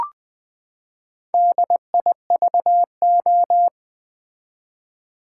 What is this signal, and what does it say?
Morse code, heard as a keyed tone, about 20 wpm, 700 Hz. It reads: DIVO
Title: Build A Word - 4 Letters At A Time - 8-Letter Words - 20wpm